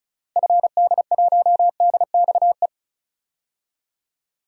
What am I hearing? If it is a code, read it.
Morse code, heard as a keyed tone, about 36 wpm, 700 Hz. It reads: FD1DXE
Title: DXCC-100 Call Signs - 6 Characters - Repeated 3x - 35wpm